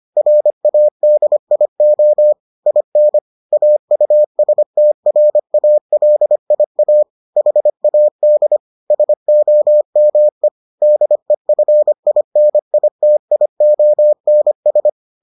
Morse code, heard as a keyed tone, about 25 wpm, 600 Hz. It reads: RADIO IN AUSTRALIA HAD SOME DEFINITIONS